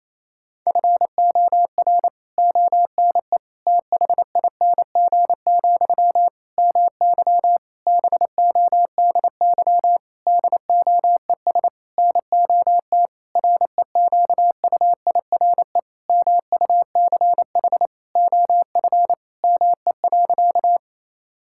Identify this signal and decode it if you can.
Morse code, heard as a keyed tone, about 28 wpm, 700 Hz. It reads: FOR ONE THING, MY BODY DOES NOT REQUIRE MUCH OF ME.
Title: Sentences from Top 1000 Words - Review Part 6 - 28wpm